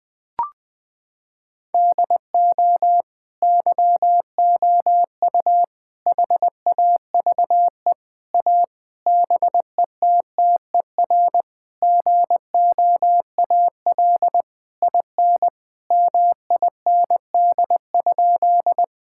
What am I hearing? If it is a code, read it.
Morse code, heard as a keyed tone, about 20 wpm, 700 Hz. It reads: DO YOU HAVE A BETTER GOAL IN MIND?
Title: Sentences from Top 1000 Words - Review Part 5 - 20wpm